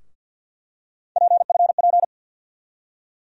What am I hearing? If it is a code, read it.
Morse code, heard as a keyed tone, about 49 wpm, 690 Hz. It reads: PFP
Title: USA Call Sign Postfixes - 3 Letters 50wpm